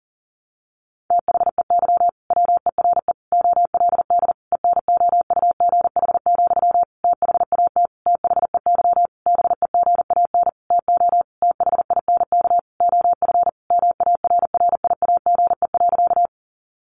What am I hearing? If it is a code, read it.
Morse code, heard as a keyed tone, about 40 wpm, 700 Hz. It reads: THEY WERE OLD ENOUGH, THAT THEY BEGAN TO THINK OF MARRIAGE.
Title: Sentences from Top 800 Words 40wpm